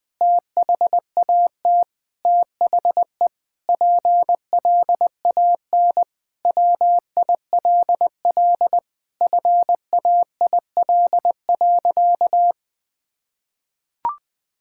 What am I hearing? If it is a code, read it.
Morse code, heard as a keyed tone, about 20 wpm, 700 Hz. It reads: THAT THE PLAN WILL FAIL.
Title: Sentences with 7-Word Phrases - 20wpm